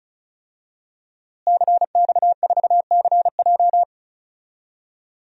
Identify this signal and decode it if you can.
Morse code, heard as a keyed tone, about 36 wpm, 700 Hz. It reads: CX4CJ
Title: DXCC-100 Call Signs - 5 Characters - Repeated 3x - 35wpm